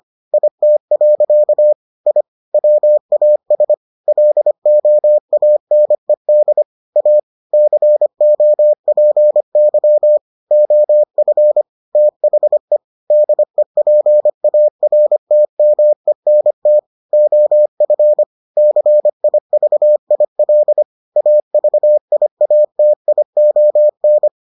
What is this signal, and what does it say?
Morse code, heard as a keyed tone, about 25 wpm, 600 Hz. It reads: IT. I WAS LOANED A COPY OF THE DEPARTMENT OF CIVIL AVIATION